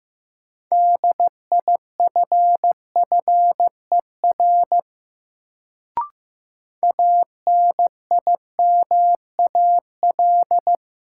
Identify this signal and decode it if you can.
Morse code, heard as a keyed tone, about 15 wpm, 700 Hz. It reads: DIFFER ANIMAL
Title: Sentences from Top 1000 Words - Review Part 1 - New Word Drill - 15wpm